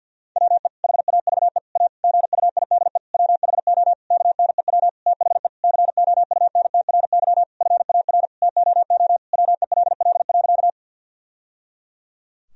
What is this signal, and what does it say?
Morse code, heard as a keyed tone, about 50 wpm, 700 Hz. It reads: WE HAVE A GUIDE WHO KNEW THE COUNTRY FAR TOO WELL.
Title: Sentences from Top 1000 Words - Review Part 3 - 50wpm